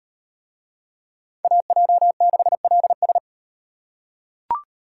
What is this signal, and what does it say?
Morse code, heard as a keyed tone, about 35 wpm, 700 Hz. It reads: AJ6LS